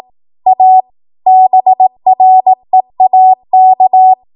Morse code, heard as a keyed tone, about 18 wpm, 750 Hz. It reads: A BREAK
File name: Morse / 18WPM